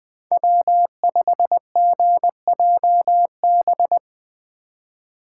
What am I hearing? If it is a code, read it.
Morse code, heard as a keyed tone, about 20 wpm, 700 Hz. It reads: W5GJB